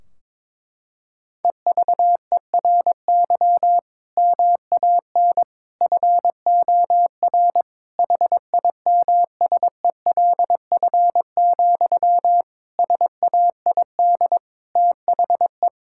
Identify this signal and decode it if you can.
Morse code, heard as a keyed tone, about 22 wpm, 700 Hz. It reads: EVERY MAN FOR HIMSELF, SAID THE
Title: Sentences from Top 1600 Words 22wpm